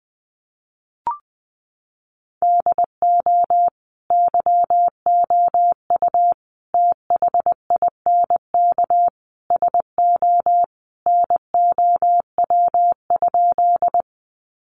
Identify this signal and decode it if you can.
Morse code, heard as a keyed tone, about 20 wpm, 700 Hz. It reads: DO YOU THINK SO NOW?
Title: Sentences from Top 200 Words 20wpm